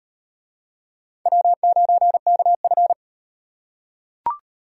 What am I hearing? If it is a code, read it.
Morse code, heard as a keyed tone, about 38 wpm, 700 Hz. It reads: W9KF